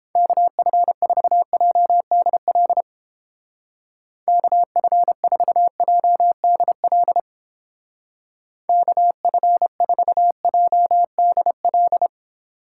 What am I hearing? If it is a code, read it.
Morse code, heard as a keyed tone, about 31 wpm, 700 Hz. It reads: KF4JDL KF4JDL KF4JDL